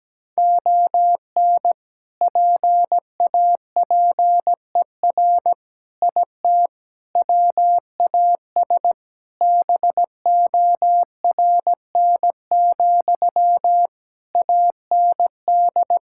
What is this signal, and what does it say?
Morse code, heard as a keyed tone, about 17 wpm, 700 Hz. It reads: ON PAPER IT WAS BORN, AND
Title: Sentences from Top 1600 Words 17wpm